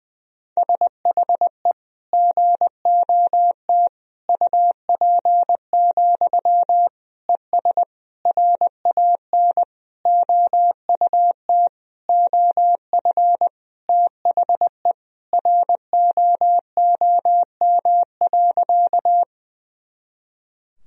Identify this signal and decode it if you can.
Morse code, heard as a keyed tone, about 20 wpm, 700 Hz. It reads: SHE GOT UP, ES RAN OUT OF THE ROOM.
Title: Sentences from Top 500 Words and Abbreviations 20wpm